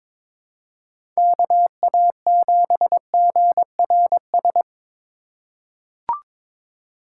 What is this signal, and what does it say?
Morse code, heard as a keyed tone, about 22 wpm, 700 Hz. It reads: KA7GRS
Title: USA Call Signs - 6 Characters - 3x Character Spacing - 22wpm